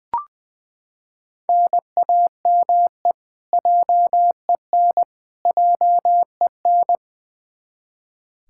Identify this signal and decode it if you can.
Morse code, heard as a keyed tone, about 20 wpm, 700 Hz. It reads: NAME JEN JEN
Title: K1USN SST QSO Practice 20wpm